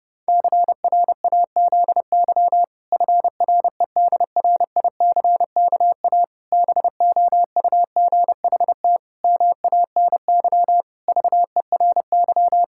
Morse code, heard as a keyed tone, about 32 wpm, 700 Hz. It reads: CRAZY FREDRICKA BOUGHT MANY VERY
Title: Pangram Challenge v2 - 30wpm